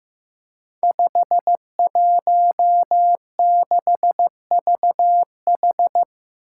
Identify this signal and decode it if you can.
Morse code, heard as a keyed tone, about 15 wpm, 700 Hz. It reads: 516VH
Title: Five Challenging Letter-Number Combinations 15wpm